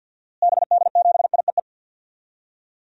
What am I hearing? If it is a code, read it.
Morse code, heard as a keyed tone, about 51 wpm, 700 Hz. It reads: BD7IEE
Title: DXCC-100 Call Signs - 6 Characters - Repeated 2x - 50wpm